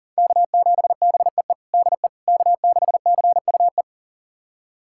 Morse code, heard as a keyed tone, about 40 wpm, 700 Hz. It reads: K7BEE DE K6CUE
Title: USA Callsign DE Callsign - 5 Characters - 40wpm